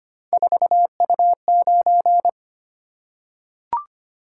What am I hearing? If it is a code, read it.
Morse code, heard as a keyed tone, about 25 wpm, 700 Hz. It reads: 4U9
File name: Three Letter-Number Combinations - 25wpm